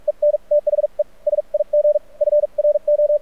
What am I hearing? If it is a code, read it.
Morse code, heard as a keyed tone, about 48 wpm, spaced as for 40 wpm, 590 Hz. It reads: EN THE SIG FRO
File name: archived / CRSnet